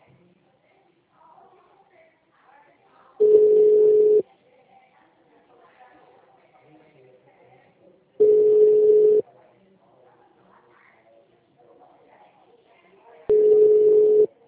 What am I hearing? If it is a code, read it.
Ringback tone (Europe, 425 Hz), with speech.